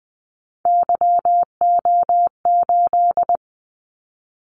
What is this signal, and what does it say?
Morse code, heard as a keyed tone, about 20 wpm, 700 Hz. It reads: YO8
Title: Callsign Prefixes for Top 25 DXCC - 20wpm